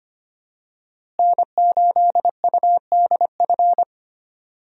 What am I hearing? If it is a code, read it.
Morse code, heard as a keyed tone, about 25 wpm, 700 Hz. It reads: N8UDF
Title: USA Call Signs - 5 Characters - Repeated 2x - 25wpm